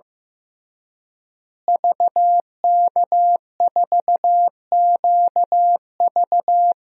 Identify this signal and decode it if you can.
Morse code, heard as a keyed tone, about 16 wpm, 700 Hz. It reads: VK4QV
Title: DXCC-100 Call Signs - 5 Characters - Repeated 3x - 15wpm